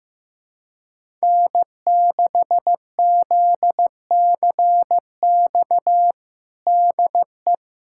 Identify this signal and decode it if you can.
Morse code, heard as a keyed tone, about 15 wpm, 700 Hz. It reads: N6ZCX DE
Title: USA Callsign DE Callsign - 5 Characters - 15wpm